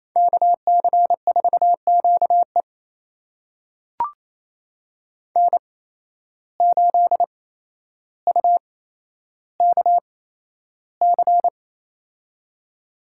Morse code, heard as a keyed tone, about 28 wpm, spaced as for 12 wpm, 700 Hz. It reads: KC4QE N8UKC
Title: USA Call Signs - 5 Characters - 8x Character Spacing - 28wpm